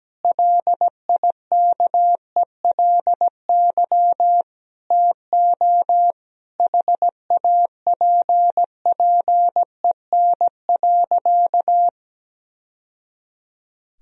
Morse code, heard as a keyed tone, about 17 wpm, 700 Hz. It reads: LIKELY TO HAPPEN.
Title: Sentences from Top 1000 Words - Review Part 4 - 17wpm